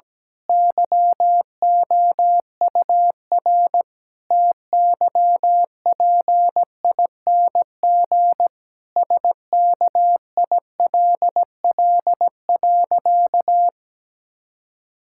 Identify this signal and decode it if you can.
Morse code, heard as a keyed tone, about 17 wpm, 700 Hz. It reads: YOUR TYPING SKILL.
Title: Pangram Challenge v2 - 17wpm